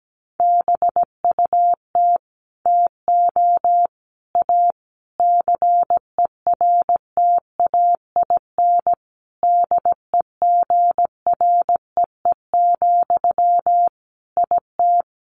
Morse code, heard as a keyed tone, about 17 wpm, 700 Hz. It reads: BUT TO A CERTAIN DEGREE, IT
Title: Sentences from Top 1500 Words 17wpm